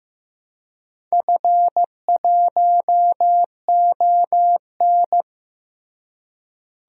Morse code, heard as a keyed tone, about 16 wpm, 700 Hz. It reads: F1ON
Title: DXCC-100 Call Signs - 4 Characters - Repeated 3x - 15wpm